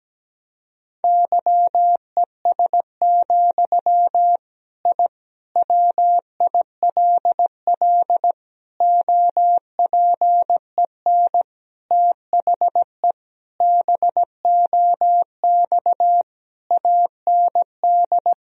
Morse code, heard as a keyed tone, about 17 wpm, 700 Hz. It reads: YES, I WILL OPEN THE BOX AND
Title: Sentences from Top 1000 Words - Review Part 2 - 17wpm